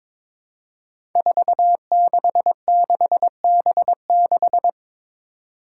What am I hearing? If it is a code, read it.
Morse code, heard as a keyed tone, about 22 wpm, 700 Hz. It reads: 466B6
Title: Five Challenging Letter-Number Combinations 22wpm